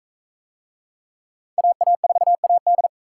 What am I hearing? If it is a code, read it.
Morse code, heard as a keyed tone, about 40 wpm, 700 Hz. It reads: AA4AD